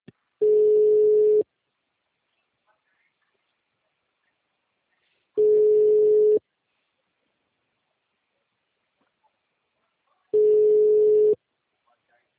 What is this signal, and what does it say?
Ringback tone (Europe, 425 Hz), with speech.